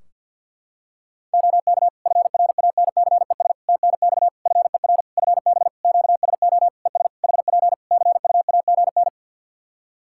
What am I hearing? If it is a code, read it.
Morse code, heard as a keyed tone, about 52 wpm, 700 Hz. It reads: OK FRANCES TNX FER FB QSO ES HP CUAGN
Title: Practice QSO - Exchange 4 - Ending - 50wpm